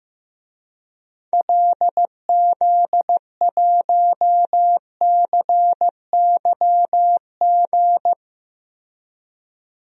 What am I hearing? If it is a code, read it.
Morse code, heard as a keyed tone, about 15 wpm, 700 Hz. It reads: LZ1CYG
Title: DXCC-100 Call Signs - 6 Characters - Repeated 2x - 15wpm